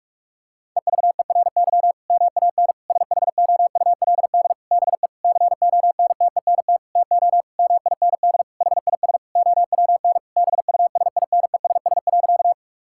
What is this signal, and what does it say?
Morse code, heard as a keyed tone, about 46 wpm, 700 Hz. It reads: EVERY MAN SHOULD BE CONTENT TO MIND HIS OWN BUSINESS.
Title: Sentences from Top 1000 Words - Review Part 6 - 45wpm